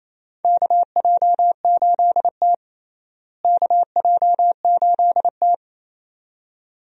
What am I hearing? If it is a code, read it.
Morse code, heard as a keyed tone, about 28 wpm, 700 Hz. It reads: KJ8T KJ8T
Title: USA Call Signs - 4 Characters - Repeated 2x - 28wpm